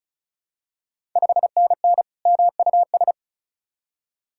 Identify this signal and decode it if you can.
Morse code, heard as a keyed tone, about 36 wpm, 700 Hz. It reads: 5NN MUS
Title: Ontario QSO Party Exchange 35wpm